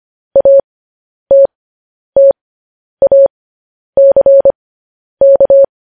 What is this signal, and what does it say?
Morse code, heard as a keyed tone, about 25 wpm, spaced as for 10 wpm, 550 Hz. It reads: ATTACK
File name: podcast / News headlines-1x-25x10